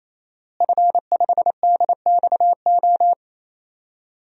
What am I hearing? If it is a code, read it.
Morse code, heard as a keyed tone, about 28 wpm, 700 Hz. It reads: F5DXO
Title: DXCC-100 Call Signs - 5 Characters - Repeated 3x - 28wpm